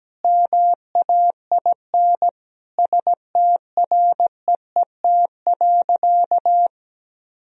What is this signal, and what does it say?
Morse code, heard as a keyed tone, about 17 wpm, 700 Hz. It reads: MAIN STREET.
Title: Sentences from Top 1000 Words - Review Part 2 - 17wpm